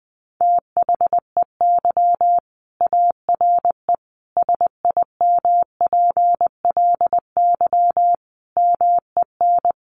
Morse code, heard as a keyed tone, about 20 wpm, 700 Hz. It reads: THEY ARE SIMPLY MEN
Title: Sentences from Top 1500 Words 20wpm